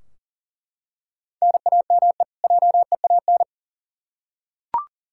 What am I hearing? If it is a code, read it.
Morse code, heard as a keyed tone, about 40 wpm, 700 Hz. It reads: NAME JEAN
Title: Practice QSO - Exchange 1 - Beginning - 40wpm